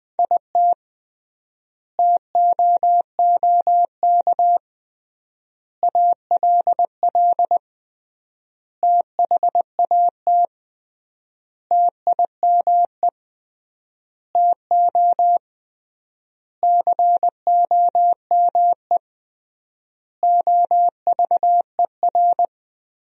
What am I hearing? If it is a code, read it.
Morse code, heard as a keyed tone, about 20 wpm, 700 Hz. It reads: IT TOOK ALL THAT TIME TO COME OVER
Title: 3x Word-Spacing - Sentences from Top 300 Words - 20wpm